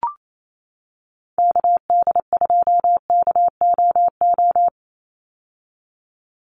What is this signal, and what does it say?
Morse code, heard as a keyed tone, about 28 wpm, 700 Hz. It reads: KD2KOO
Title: USA Call Signs 28wpm